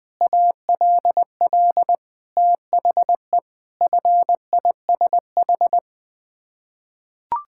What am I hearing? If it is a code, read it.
Morse code, heard as a keyed tone, about 20 wpm, 700 Hz. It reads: ALL THE FISH
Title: Sets of 8 Words - 20wpm